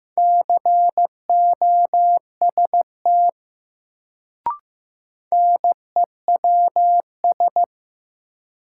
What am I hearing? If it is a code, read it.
Morse code, heard as a keyed tone, about 15 wpm, 700 Hz. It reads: COST NEWS
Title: Sentences from Top 800 Words - New Word Drill - 15wpm